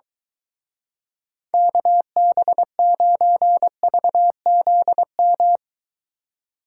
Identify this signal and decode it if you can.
Morse code, heard as a keyed tone, about 23 wpm, 700 Hz. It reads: KB9VZM